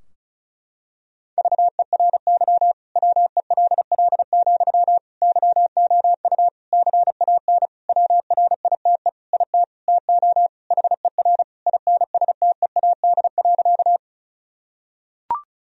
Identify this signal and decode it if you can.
Morse code, heard as a keyed tone, about 34 wpm, 700 Hz. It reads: VERY WELL, YOU CAN WRITE IT TO HER INSTEAD.
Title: Sentences from Top 1000 Words - Review Part 4 - 35wpm